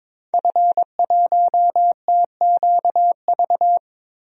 Morse code, heard as a keyed tone, about 22 wpm, 700 Hz. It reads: F1TQV
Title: DXCC-100 Call Signs - 5 Characters - Repeated 2x - 22wpm